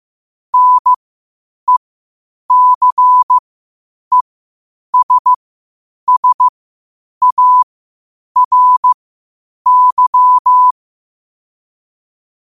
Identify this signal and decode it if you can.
Morse code, heard as a keyed tone, about 15 wpm, spaced as for 8 wpm, 1000 Hz. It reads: NECESSARY
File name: qotdfiles / 8wpm